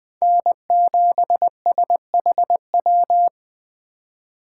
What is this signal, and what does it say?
Morse code, heard as a keyed tone, about 20 wpm, 700 Hz. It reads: N7SHW